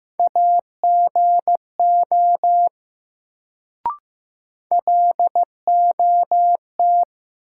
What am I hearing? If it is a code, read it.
Morse code, heard as a keyed tone, about 15 wpm, 700 Hz. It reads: AGO LOT
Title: Sentences from Top 500 Words - New Word Drill - 15wpm